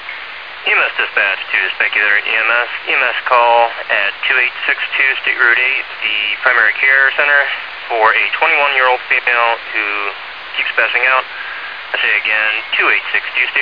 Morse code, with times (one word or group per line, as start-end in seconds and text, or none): none